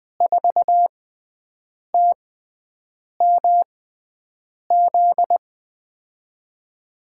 Morse code, heard as a keyed tone, about 20 wpm, spaced as for 7 wpm, 700 Hz. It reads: 4TMZ
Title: USA Call Signs - 6x Character Spacing - 20wpm